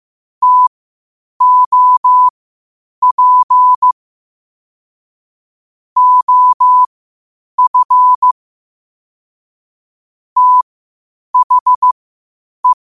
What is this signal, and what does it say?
Morse code, heard as a keyed tone, about 15 wpm, spaced as for 8 wpm, 1000 Hz. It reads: TOP OF THE
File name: qotdfiles / 8wpm